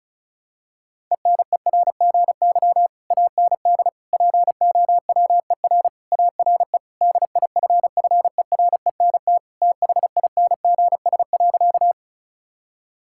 Morse code, heard as a keyed tone, about 34 wpm, 700 Hz. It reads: ENERGY AND POWER ARE DIFFERENT THINGS.
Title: Sentences from Top 1000 Words - Review Part 6 - 35wpm